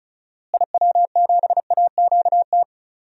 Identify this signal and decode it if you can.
Morse code, heard as a keyed tone, about 36 wpm, 700 Hz. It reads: IW7AQT
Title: DXCC-100 Call Signs - 6 Characters - Repeated 2x - 35wpm